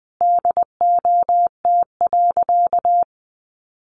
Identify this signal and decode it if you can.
Morse code, heard as a keyed tone, about 19 wpm, 700 Hz. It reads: DOT.
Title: Sentences from Top 1500 Words 20wpm